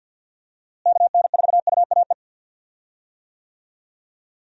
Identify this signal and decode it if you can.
Morse code, heard as a keyed tone, about 51 wpm, 700 Hz. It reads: KN4UAE
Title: USA Call Signs - 6 Characters - Repeated 3x - 50wpm